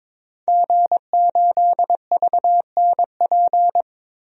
Morse code, heard as a keyed tone, about 22 wpm, 700 Hz. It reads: G8VNP
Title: DXCC-100 Call Signs - 5 Characters - Repeated 3x - 22wpm